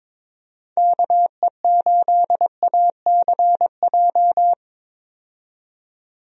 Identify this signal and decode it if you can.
Morse code, heard as a keyed tone, about 22 wpm, 700 Hz. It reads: KE8ACJ
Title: USA Call Signs - 6 Characters - 22wpm